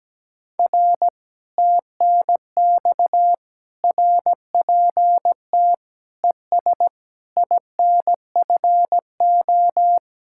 Morse code, heard as a keyed tone, about 18 wpm, 700 Hz. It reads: R TNX RPT ES INFO
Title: Speed-Racing Common QSO Phrases - 17wpm